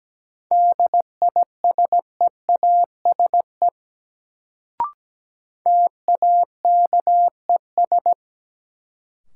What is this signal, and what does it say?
Morse code, heard as a keyed tone, about 17 wpm, 700 Hz. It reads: DISEASE TAKES
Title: Sentences from Top 1000 Words - Review Part 6 - New Word Drill - 17wpm